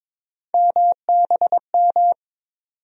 Morse code, heard as a keyed tone, about 22 wpm, 700 Hz. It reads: MBM
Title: USA Call Sign Postfixes - 3 Letters 22wpm